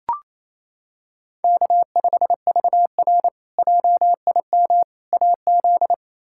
Morse code, heard as a keyed tone, about 28 wpm, 700 Hz. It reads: K5VR JIM AZ
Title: K1USN SST QSO Practice - 28wpm